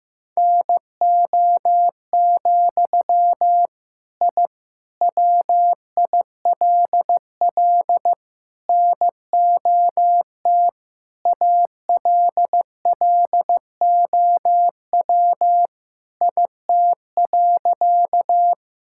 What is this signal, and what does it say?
Morse code, heard as a keyed tone, about 15 wpm, 700 Hz. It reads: NO, I WILL NOT ALLOW IT.
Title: Sentences from Top 1000 Words - Review Part 5 - 15wpm